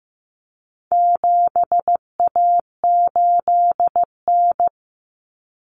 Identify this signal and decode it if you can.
Morse code, heard as a keyed tone, about 15 wpm, 700 Hz. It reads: 7A8N
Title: Common Cut Numbers - 4-Digit 15wpm